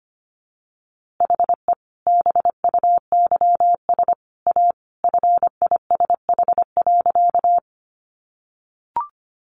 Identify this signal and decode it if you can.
Morse code, heard as a keyed tone, about 25 wpm, 700 Hz. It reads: HE BUYS A FISH.
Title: Easy Phrases 25wpm